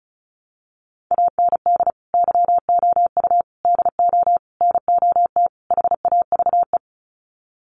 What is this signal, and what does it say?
Morse code, heard as a keyed tone, about 34 wpm, 700 Hz. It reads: AND YOU DO NOT HAVE
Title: Sets of 5 Words Encore 35wpm